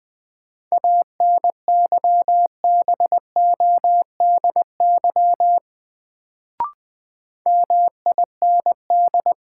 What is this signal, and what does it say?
Morse code, heard as a keyed tone, about 20 wpm, 700 Hz. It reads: ANYBODY MIND
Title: Sets of 5 Words - New Word Drill - 20wpm